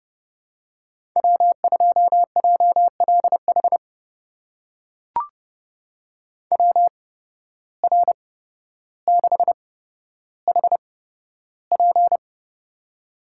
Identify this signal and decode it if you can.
Morse code, heard as a keyed tone, about 29 wpm, spaced as for 13 wpm, 700 Hz. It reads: W2JLH WR6HP
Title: USA Call Signs - 5 Characters - 8x Character Spacing - 30wpm